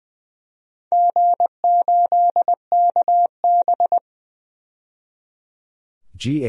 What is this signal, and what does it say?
Morse code, heard as a keyed tone, about 20 wpm, 700 Hz. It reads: G8KB
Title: DXCC-100 Call Signs - 4 Characters - 20wpm